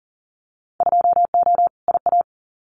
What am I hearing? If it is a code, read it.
Morse code, heard as a keyed tone, about 41 wpm, 700 Hz. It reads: 2O IA
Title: Winter Field Day Exchanges 40wpm